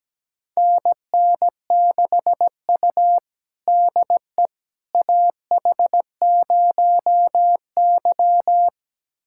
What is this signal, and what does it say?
Morse code, heard as a keyed tone, about 17 wpm, 700 Hz. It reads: NN6U DE AH0Y
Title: USA Callsign DE Callsign - 4 Characters - 17wpm